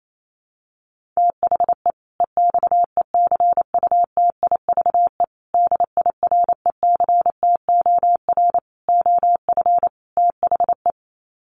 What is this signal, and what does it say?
Morse code, heard as a keyed tone, about 28 wpm, 700 Hz. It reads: THE EXECUTIVE DIRECTOR OF THE
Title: Sets of 5 Words Encore 28wpm